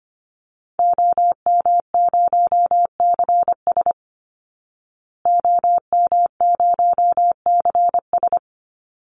Morse code, heard as a keyed tone, about 25 wpm, 700 Hz. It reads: OM0CS OM0CS
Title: Call Signs Repeated 2x - 25wpm